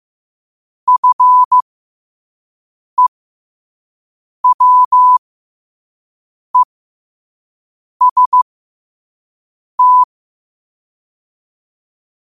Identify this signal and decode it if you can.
Morse code, heard as a keyed tone, about 15 wpm, spaced as for 5 wpm, 1000 Hz. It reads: FEWEST
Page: Quotes for Wed, 13 Aug 2025 in Morse Code at 5 words per minute.